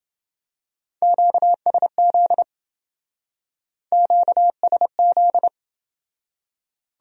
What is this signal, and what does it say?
Morse code, heard as a keyed tone, about 27 wpm, 700 Hz. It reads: QSZ QSZ